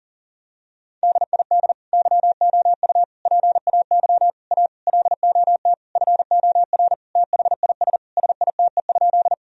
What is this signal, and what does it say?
Morse code, heard as a keyed tone, about 40 wpm, 700 Hz. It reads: DID YOU PAY A LOT FOR THIS SITE?
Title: Sentences from Top 1000 Words - Review Part 4 - 40wpm